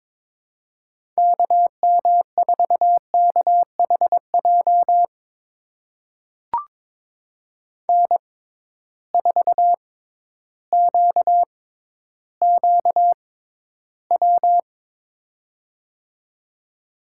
Morse code, heard as a keyed tone, about 22 wpm, spaced as for 12 wpm, 700 Hz. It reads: KM4KHJ N4QQW
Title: USA Call Signs - 6x Character Spacing - 22wpm